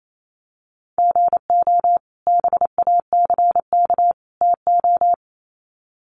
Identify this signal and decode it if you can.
Morse code, heard as a keyed tone, about 28 wpm, 700 Hz. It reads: GO BACK TO
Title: Sets of 3 Words 28wpm